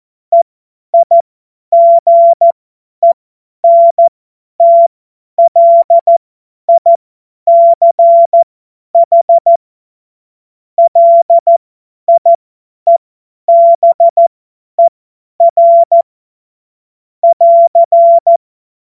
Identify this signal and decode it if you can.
Morse code, heard as a keyed tone, about 14 wpm, spaced as for 10 wpm, 680 Hz. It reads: EIGENTLICH LIEBER <AR>
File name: Morsekurs / Witze / Witz 23 Januar 2024 50 F70